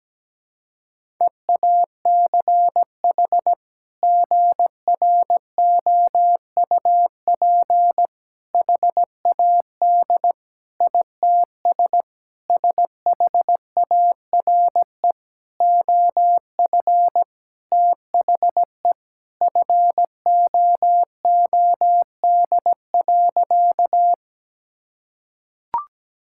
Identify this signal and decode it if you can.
Morse code, heard as a keyed tone, about 17 wpm, 700 Hz. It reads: EACH GROUP HAD ITS SHARE OF THE FOOD.
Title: Sentences from Top 1000 Words - Review Part 6 - 17wpm